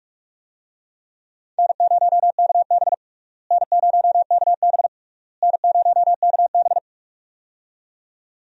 Morse code, heard as a keyed tone, about 41 wpm, 700 Hz. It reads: N0KB N0KB N0KB
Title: USA Call Signs - 4 Characters - Repeated 3x - 45wpm